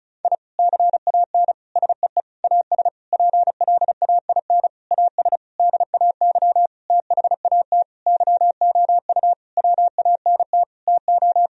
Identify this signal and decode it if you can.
Morse code, heard as a keyed tone, about 36 wpm, 700 Hz. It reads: I CAN SEE AS PLAIN AS DAY THAT YOU WANT TO
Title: Sentences from Top 1000 Words - Review Part 2 - 35wpm